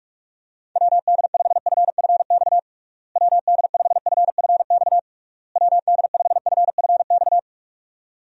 Morse code, heard as a keyed tone, about 42 wpm, 700 Hz. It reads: WD5FFX WD5FFX WD5FFX
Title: USA Call Signs - 6 Characters - Repeated 3x - 45wpm